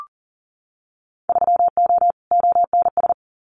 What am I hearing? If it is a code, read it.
Morse code, heard as a keyed tone, about 38 wpm, 700 Hz. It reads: 3O ONS
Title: Winter Field Day Exchanges Encore 40wpm